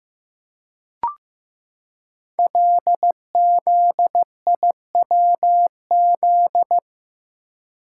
Morse code, heard as a keyed tone, about 15 wpm, 700 Hz. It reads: LZIWZ
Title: Code Groups - 5 Letters - 15wpm